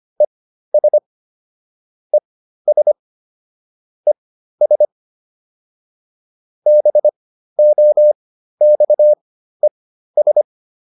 Morse code, heard as a keyed tone, about 25 wpm, spaced as for 13 wpm, 600 Hz. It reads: ES ES ES BOXES